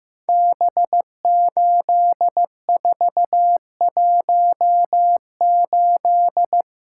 Morse code, heard as a keyed tone, about 15 wpm, 700 Hz. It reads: B8418
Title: Code Groups - 5 Letter:Numbers - 15wpm